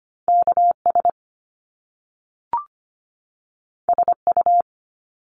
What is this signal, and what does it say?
Morse code, heard as a keyed tone, about 25 wpm, 700 Hz. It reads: KS SU
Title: Top Two Letter Combinations in Words 25wpm